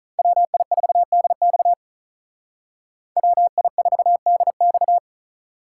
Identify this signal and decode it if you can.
Morse code, heard as a keyed tone, about 42 wpm, 700 Hz. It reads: WI4DX WI4DX
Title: Speed Racing - USA Call Signs 35wpm